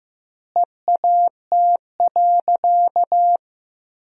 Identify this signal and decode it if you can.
Morse code, heard as a keyed tone, about 15 wpm, 700 Hz. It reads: EAT.
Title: 2x Word-Spacing - Sentences from Top 300 Words - 15wpm